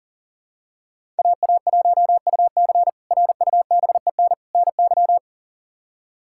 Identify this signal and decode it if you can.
Morse code, heard as a keyed tone, about 40 wpm, 700 Hz. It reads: AA1UC RUBEN NY
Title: K1USN SST QSO Practice - 40wpm